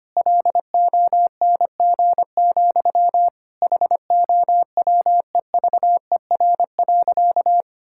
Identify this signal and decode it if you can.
Morse code, heard as a keyed tone, about 25 wpm, 700 Hz. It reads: LONG, HOWEVER.
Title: Sentences from Top 1000 Words - Review Part 6 - 25wpm